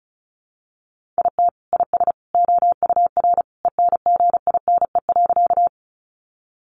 Morse code, heard as a keyed tone, about 34 wpm, 700 Hz. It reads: IT IS OUR ENGINE.
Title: Sentences from Top 1500 Words 35wpm